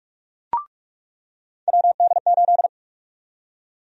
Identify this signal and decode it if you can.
Morse code, heard as a keyed tone, about 42 wpm, 700 Hz. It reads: WD8
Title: USA Call Sign Prefixes - 45wpm